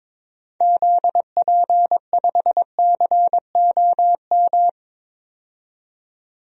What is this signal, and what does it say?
Morse code, heard as a keyed tone, about 21 wpm, 700 Hz. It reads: ZP5COM
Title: DXCC-100 Call Signs - 6 Characters - 22wpm